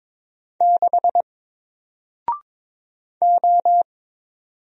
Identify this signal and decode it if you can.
Morse code, heard as a keyed tone, about 22 wpm, 700 Hz. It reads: 6 O
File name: Single-Letter-Number-22wpm